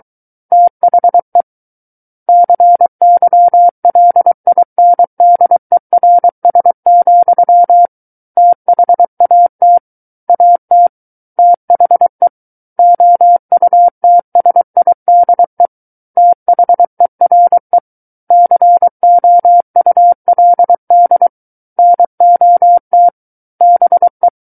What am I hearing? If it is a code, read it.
Morse code, heard as a keyed tone, about 23 wpm, 700 Hz. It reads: THE CYLINDERS, THAT AT THE OUTSIDE THERE COULD NOT BE
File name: Book One / 14-Chapter 14 - 23 WPM